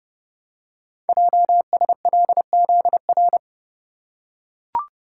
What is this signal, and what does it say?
Morse code, heard as a keyed tone, about 30 wpm, 700 Hz. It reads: JSLZR
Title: Code Groups - 5 Letters - 30wpm